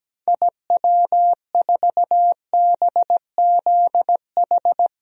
Morse code, heard as a keyed tone, about 17 wpm, 700 Hz. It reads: IW4BZH
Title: DXCC-100 Call Signs - 6 Characters - 17wpm